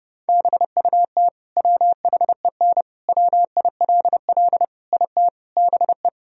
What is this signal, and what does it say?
Morse code, heard as a keyed tone, about 30 wpm, 700 Hz. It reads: BUT WHEN WILL IT BE
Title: Sentences from Top 1000 Words - Review Part 3 - 30wpm